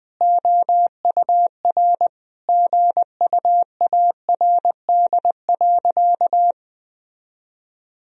Morse code, heard as a keyed tone, about 19 wpm, 700 Hz. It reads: OUR GUARD.
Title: Sentences from Top 1000 Words - Review Part 4 - 20wpm